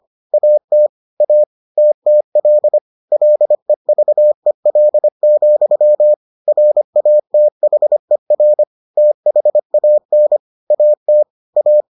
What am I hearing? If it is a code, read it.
Morse code, heard as a keyed tone, about 25 wpm, 600 Hz. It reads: AT A TTL LEVEL, RATHER THAN AT A